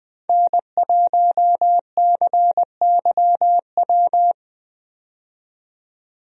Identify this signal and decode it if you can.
Morse code, heard as a keyed tone, about 20 wpm, 700 Hz. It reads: N1CYW
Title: USA Call Signs - 5 Characters - 20wpm